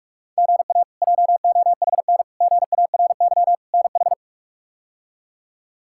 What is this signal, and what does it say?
Morse code, heard as a keyed tone, about 41 wpm, 700 Hz. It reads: GA JOHN GARY NH
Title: K1USN SST QSO Practice - 45wpm